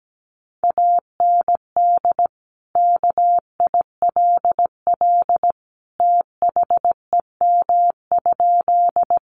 Morse code, heard as a keyed tone, about 17 wpm, 700 Hz. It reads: AND KILL THEM?
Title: Sentences from Top 1400 Words 17wpm